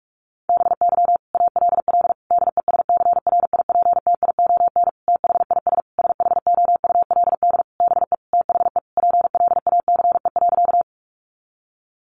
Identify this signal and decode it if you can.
Morse code, heard as a keyed tone, about 44 wpm, 700 Hz. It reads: BY ALL DESCRIPTION THIS SHOULD BE THE PLACE.
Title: Sentences from Top 1300 Words 45wpm